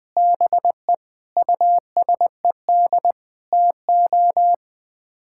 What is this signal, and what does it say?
Morse code, heard as a keyed tone, about 20 wpm, 700 Hz. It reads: BE USED TO
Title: Build A Phrase - 4-Word Phrases - 20wpm